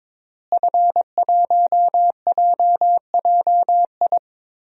Morse code, heard as a keyed tone, about 22 wpm, 700 Hz. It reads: F1JJI
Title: DXCC-100 Call Signs - 5 Characters - Repeated 3x - 22wpm